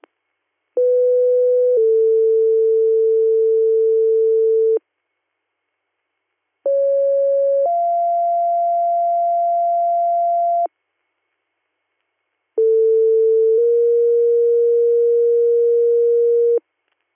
Two-tone page, heard as a one-second tone, then a three-second tone, about 500 then 450 Hz, about 550 then 690 Hz, about 450 then 470 Hz. It is heard from 0.8 to 16.6 s.